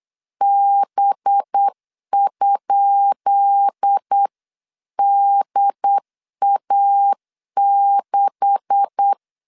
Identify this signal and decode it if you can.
Morse code, heard as a keyed tone, about 8 wpm, 800 Hz. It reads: B?DA6
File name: LEZIONI / 10 SIGNS / 10B INT 8wpm 3534